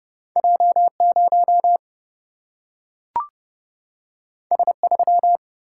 Morse code, heard as a keyed tone, about 30 wpm, 700 Hz. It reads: J0 S3
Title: Single-Letter Single-Number 30wpm